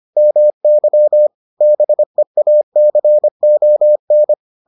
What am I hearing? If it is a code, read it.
Morse code, heard as a keyed tone, about 25 wpm, 600 Hz. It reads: MY BEACON